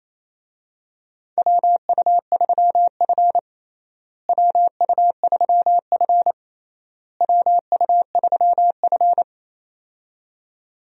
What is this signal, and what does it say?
Morse code, heard as a keyed tone, about 28 wpm, 700 Hz. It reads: WU3F WU3F WU3F
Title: USA Call Signs - 4 Characters - Repeated 3x - 28wpm